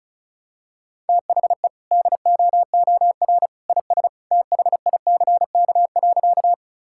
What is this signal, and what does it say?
Morse code, heard as a keyed tone, about 36 wpm, 700 Hz. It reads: THE DOOR IS THICK.
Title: Sentences from Top 1600 Words 35wpm